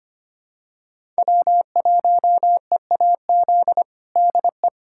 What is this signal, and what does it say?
Morse code, heard as a keyed tone, about 25 wpm, 700 Hz. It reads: W1EAZ DE
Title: USA Callsign DE Callsign 25wpm